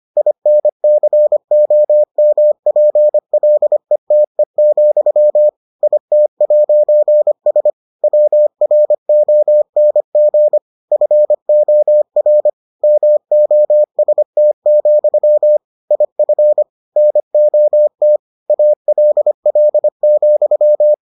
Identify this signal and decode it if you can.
Morse code, heard as a keyed tone, about 25 wpm, 600 Hz. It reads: INCOMPLETE, IT'S WRONG FOR MOST, IF NOT ALL,